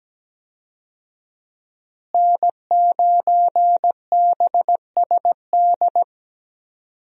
Morse code, heard as a keyed tone, about 17 wpm, 700 Hz. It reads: N9BSD
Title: USA Call Signs - 5 Characters - Repeated 2x - 17wpm